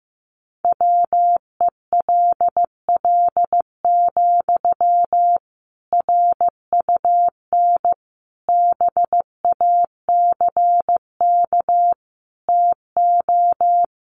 Morse code, heard as a keyed tone, about 15 wpm, 700 Hz. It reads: WELL, RUN BACK TO
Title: Sentences from Top 300 Words 15wpm